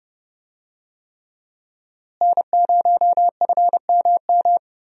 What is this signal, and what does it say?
Morse code, heard as a keyed tone, about 30 wpm, 700 Hz. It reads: N0FMM
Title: USA Call Signs - 5 Characters - Repeated 3x - 30wpm